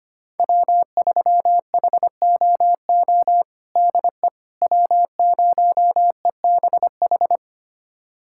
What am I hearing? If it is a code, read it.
Morse code, heard as a keyed tone, about 25 wpm, 700 Hz. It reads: W3HOO DE W0EBH
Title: USA Callsign DE Callsign - 5 Characters - 25wpm